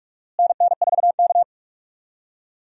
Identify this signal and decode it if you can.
Morse code, heard as a keyed tone, about 43 wpm, 700 Hz. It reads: NN4K
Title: USA Call Signs - 4 Characters - Repeated 3x - 45wpm